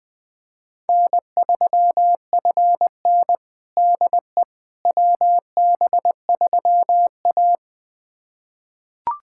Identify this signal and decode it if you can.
Morse code, heard as a keyed tone, about 20 wpm, 700 Hz. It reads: N3FN DE WB3A
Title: USA Callsign DE Callsign - 4 Characters - 20wpm